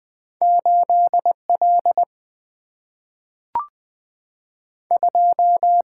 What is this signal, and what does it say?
Morse code, heard as a keyed tone, about 20 wpm, 700 Hz. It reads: 8 L 2
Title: Two Letter-Number Combinations - 20wpm